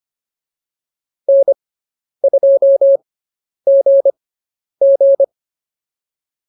Morse code, heard as a keyed tone, about 25 wpm, spaced as for 10 wpm, 550 Hz. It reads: N2GG